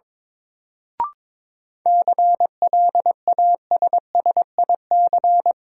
Morse code, heard as a keyed tone, about 22 wpm, 700 Hz. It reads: CLASSIC
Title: Sentences from Top 1000 Words - New Word Drill - 22wpm